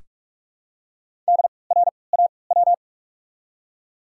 Morse code, heard as a keyed tone, about 47 wpm, spaced as for 26 wpm, 700 Hz. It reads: DRAW
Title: 3x Character Spacing - Top 300 Words - 45wpm